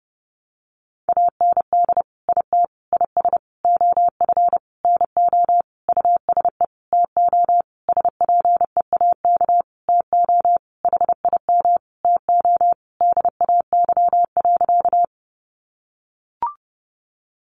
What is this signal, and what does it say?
Morse code, heard as a keyed tone, about 30 wpm, 700 Hz. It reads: AND IT IS OF NO USE TO SPEAK TO HIM TO DAY.
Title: Sentences from Top 1100 Words 30wpm